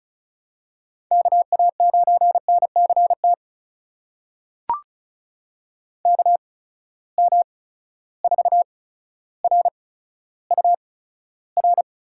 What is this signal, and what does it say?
Morse code, heard as a keyed tone, about 36 wpm, spaced as for 15 wpm, 700 Hz. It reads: KA9NCT KM4RUR
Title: USA Call Signs - 6 Characters - 8x Character Spacing - 35wpm